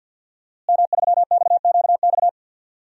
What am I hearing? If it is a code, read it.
Morse code, heard as a keyed tone, about 45 wpm, 700 Hz. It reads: M3XQX
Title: DXCC-100 Call Signs - 5 Characters - Repeated 3x - 50wpm